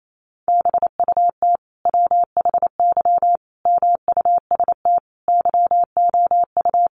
Morse code, heard as a keyed tone, about 28 wpm, 700 Hz. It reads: BUT WHY MUST YOU
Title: Sentences from Top 400 Words 28wpm